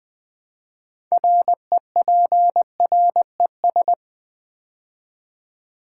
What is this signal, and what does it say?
Morse code, heard as a keyed tone, about 20 wpm, 700 Hz. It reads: REPRES
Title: Build A Word - 3 Letters At A Time - 9-Letter Words - 20wpm